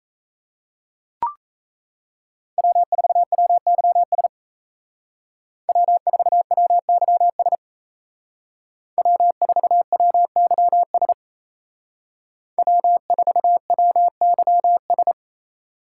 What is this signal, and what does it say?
Morse code, heard as a keyed tone, about 33 wpm, 700 Hz. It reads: W4WYS W4WYS W4WYS W4WYS
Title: Speed Racing - USA 1x3 Call Signs 28wpm